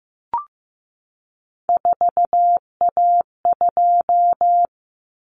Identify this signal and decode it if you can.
Morse code, heard as a keyed tone, about 15 wpm, 700 Hz. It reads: 4A2
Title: Common Cut Numbers - 3-Digit 15wpm